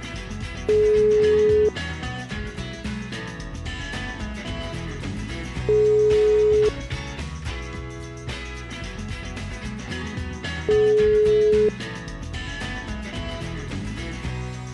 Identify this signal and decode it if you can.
Ringback tone (Europe, 425 Hz).